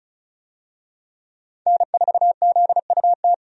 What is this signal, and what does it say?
Morse code, heard as a keyed tone, about 34 wpm, 700 Hz. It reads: N4ZUT
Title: USA Call Signs - 5 Characters - 35wpm